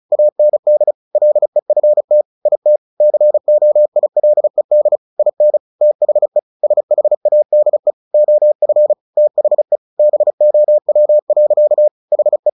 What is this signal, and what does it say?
Morse code, heard as a keyed tone, about 34 wpm, 600 Hz. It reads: AND LEFT IT COILED IN THE SHADE OF THE BOW. HE
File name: the old man and the sea / the old man and the sea 1 35wpm